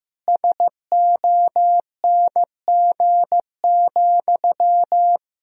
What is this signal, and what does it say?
Morse code, heard as a keyed tone, about 15 wpm, 700 Hz. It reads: SONG,
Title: Sentences from Top 1000 Words - Review Part 2 - 15wpm